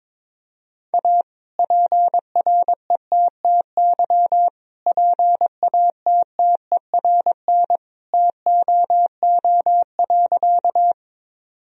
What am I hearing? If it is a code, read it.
Morse code, heard as a keyed tone, about 22 wpm, 700 Hz. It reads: A PRETTY PATTERN TOO.
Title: Sentences from Top 1000 Words - Review Part 2 - 22wpm